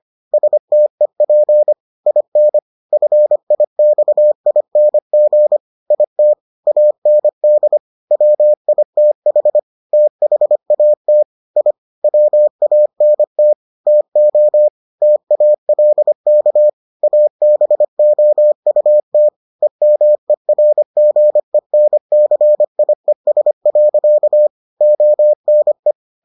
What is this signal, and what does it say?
Morse code, heard as a keyed tone, about 25 wpm, 600 Hz. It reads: STEP IN FIXING IT AND WITH THAT I WANT TO TALK ABOUT EMERGENCIES. ONE